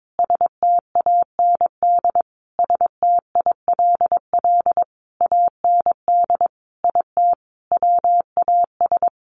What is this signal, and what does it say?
Morse code, heard as a keyed tone, about 22 wpm, 700 Hz. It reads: STAND STILL AND IT WAS
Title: Sentences from Top 800 Words 22wpm